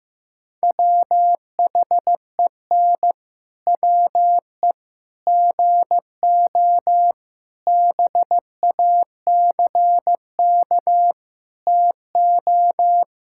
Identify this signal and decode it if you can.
Morse code, heard as a keyed tone, about 15 wpm, 700 Hz. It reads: WHEN WE GO BACK TO
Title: Pangram Challenge v2 - 15wpm